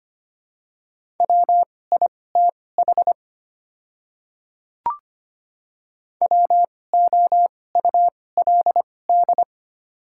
Morse code, heard as a keyed tone, about 25 wpm, spaced as for 18 wpm, 690 Hz. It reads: WITH WOULD
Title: 2x Character Spacing - Top 100 Words - 25wpm